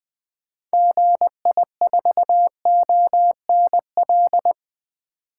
Morse code, heard as a keyed tone, about 20 wpm, 700 Hz. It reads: GI4ONL
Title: 2023 SOTA Activator Call Signs - 20wpm